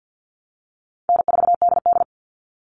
Morse code, heard as a keyed tone, about 50 wpm, 700 Hz. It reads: N4DD
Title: Call Signs Repeated 2x - 50wpm